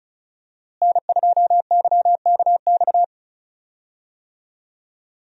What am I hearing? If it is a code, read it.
Morse code, heard as a keyed tone, about 36 wpm, 700 Hz. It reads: N2YKX
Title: USA Call Signs - 5 Characters - Repeated 2x - 35wpm